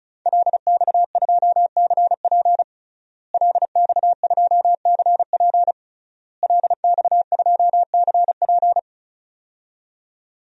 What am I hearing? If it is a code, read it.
Morse code, heard as a keyed tone, about 34 wpm, 700 Hz. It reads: LX2CP LX2CP LX2CP
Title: DXCC-100 Call Signs - 5 Characters - Repeated 3x - 35wpm